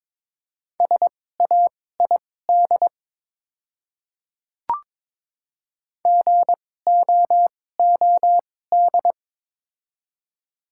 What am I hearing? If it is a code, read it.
Morse code, heard as a keyed tone, about 23 wpm, spaced as for 16 wpm, 700 Hz. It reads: SAID GOOD
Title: 2x Character Spacing - Top 200 Words - 22wpm